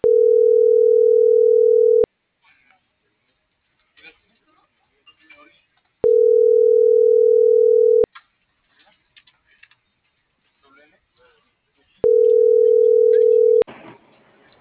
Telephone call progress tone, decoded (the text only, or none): ringback tone